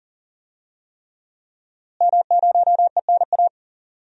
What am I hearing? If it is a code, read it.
Morse code, heard as a keyed tone, about 37 wpm, 700 Hz. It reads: M0ENA
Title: DXCC-100 Call Signs - 5 Characters - Repeated 3x - 40wpm